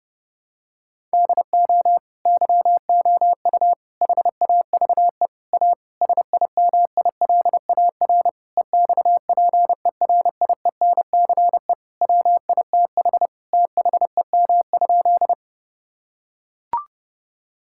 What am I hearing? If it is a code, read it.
Morse code, heard as a keyed tone, about 28 wpm, 700 Hz. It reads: DO YOU HAVE A SIMILAR EXPERIENCE WITH THEM?
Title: Sentences from Top 1000 Words - Review Part 5 - 30wpm